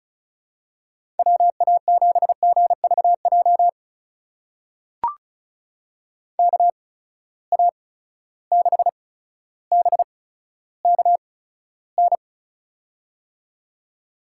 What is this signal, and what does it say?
Morse code, heard as a keyed tone, about 36 wpm, spaced as for 15 wpm, 700 Hz. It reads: WA7GVJ KA6BKN
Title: USA Call Signs - 6 Characters - 8x Character Spacing - 35wpm